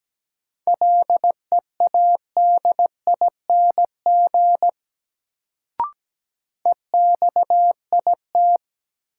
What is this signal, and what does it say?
Morse code, heard as a keyed tone, about 17 wpm, 700 Hz. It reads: LEADING EXIT
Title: Sentences from Top 1100 Words - New Word Drill Alt - 17wpm